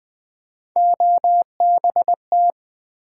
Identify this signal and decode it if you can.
Morse code, heard as a keyed tone, about 20 wpm, 700 Hz. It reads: OBT
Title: Build A Word - 3 Letters At A Time - 6-Letter Words - 20wpm